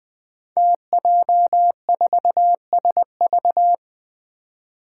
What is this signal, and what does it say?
Morse code, heard as a keyed tone, about 20 wpm, 700 Hz. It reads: TJ4SV
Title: Five Challenging Letter-Number Combinations 20wpm